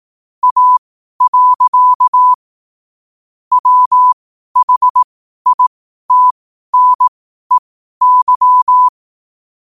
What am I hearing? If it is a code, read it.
Morse code, heard as a keyed tone, about 18 wpm, spaced as for 12 wpm, 1000 Hz. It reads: A. WHITNEY
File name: qotdfiles / 12wpm